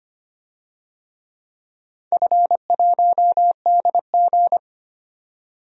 Morse code, heard as a keyed tone, about 25 wpm, 700 Hz. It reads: F1DG
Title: DXCC-100 Call Signs - 4 Characters - 25wpm